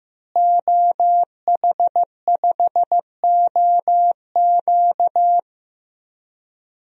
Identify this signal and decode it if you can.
Morse code, heard as a keyed tone, about 16 wpm, 700 Hz. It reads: OH5OQ
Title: DXCC-100 Call Signs - 5 Characters - Repeated 3x - 15wpm